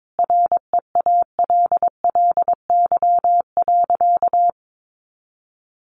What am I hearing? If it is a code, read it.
Morse code, heard as a keyed tone, about 22 wpm, 700 Hz. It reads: REALLY.
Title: Sentences from Top 600 Words 22wpm